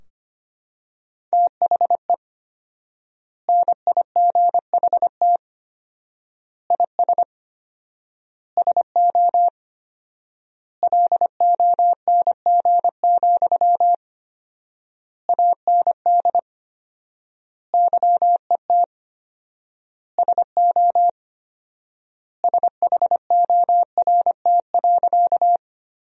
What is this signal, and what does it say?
Morse code, heard as a keyed tone, about 25 wpm, 700 Hz. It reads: THE NIGHT IS SO LONG, AND YET SO SHORT.
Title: 4x Word-Spacing - Sentences from Top 500 Words - 25wpm